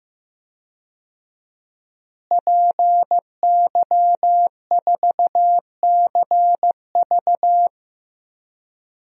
Morse code, heard as a keyed tone, about 15 wpm, 700 Hz. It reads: PY4CV
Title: DXCC-100 Call Signs - 5 Characters - Repeated 2x - 15wpm